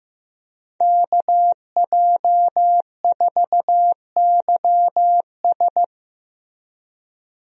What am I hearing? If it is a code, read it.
Morse code, heard as a keyed tone, about 14 wpm, 700 Hz. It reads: KJ4YS
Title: USA Call Signs - 5 Characters - Repeated 3x - 15wpm